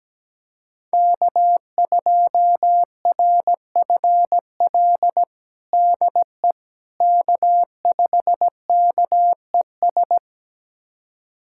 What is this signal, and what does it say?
Morse code, heard as a keyed tone, about 17 wpm, 700 Hz. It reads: K2RFL DE K5KES
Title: USA Callsign DE Callsign - 5 Characters - 17wpm